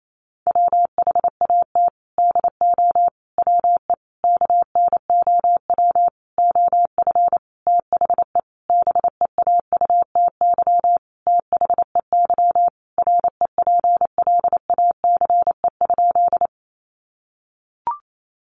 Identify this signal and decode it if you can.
Morse code, heard as a keyed tone, about 28 wpm, 700 Hz. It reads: WHAT DO WE KNOW OF THE BEAUTY THEY REPLACE?
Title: Sentences from Top 1400 Words 28wpm